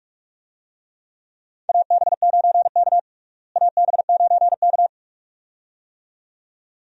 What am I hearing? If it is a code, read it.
Morse code, heard as a keyed tone, about 46 wpm, 700 Hz. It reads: AB9K AB9K
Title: USA Call Signs - 4 Characters - Repeated 2x - 45wpm